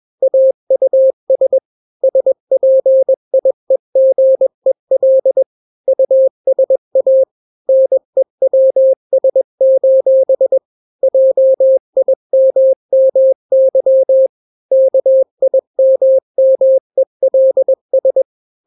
Morse code, heard as a keyed tone, about 21 wpm, 530 Hz. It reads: AUS SPIEGEL USA NEWS: JIMMY KIMMELS
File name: daily / 20wpm